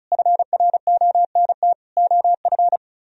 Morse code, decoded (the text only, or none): FRONT OF